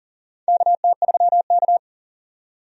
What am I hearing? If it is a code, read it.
Morse code, heard as a keyed tone, about 38 wpm, 690 Hz. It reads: KT3K
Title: USA Call Signs - 4 Characters - 40wpm